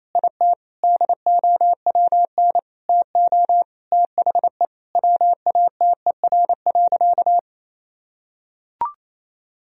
Morse code, heard as a keyed tone, about 28 wpm, 700 Hz. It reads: IT DOWN TO THE WATER.
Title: Sentences from Top 1000 Words - Review Part 4 - 28wpm